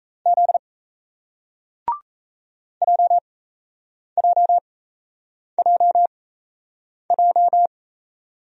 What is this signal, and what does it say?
Morse code, heard as a keyed tone, about 38 wpm, spaced as for 8 wpm, 700 Hz. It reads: Z JJJJ